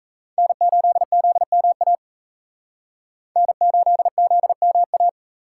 Morse code, heard as a keyed tone, about 36 wpm, 700 Hz. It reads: N8ZMA N8ZMA